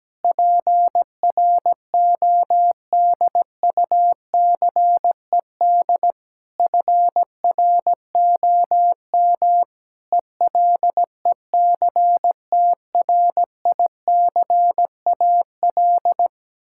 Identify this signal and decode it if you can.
Morse code, heard as a keyed tone, about 17 wpm, 700 Hz. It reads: PRODUCED FROM ELECTRICAL
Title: Sentences from Top 1000 Words - Review Part 6 - 17wpm